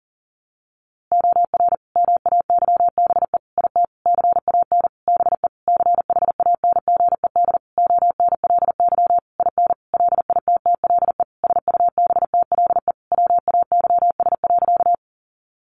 Morse code, heard as a keyed tone, about 37 wpm, 700 Hz. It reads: OR MAYBE IT CAN BE CHANGED ONLY IN LITTLE SUBTLE WAYS.
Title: A Sound Of Thunder 40wpm